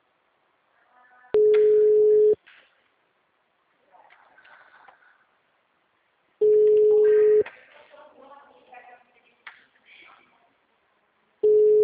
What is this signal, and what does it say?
Ringback tone (Europe, 425 Hz).